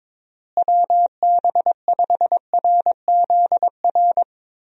Morse code, heard as a keyed tone, about 22 wpm, 700 Hz. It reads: WB5RZR
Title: USA Call Signs - 6 Characters - Repeated 3x - 22wpm